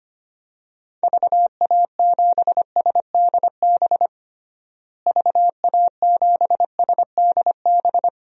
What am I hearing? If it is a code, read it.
Morse code, heard as a keyed tone, about 25 wpm, 700 Hz. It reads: VA7SDB VA7SDB
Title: DXCC-100 Call Signs - 6 Characters - Repeated 2x - 25wpm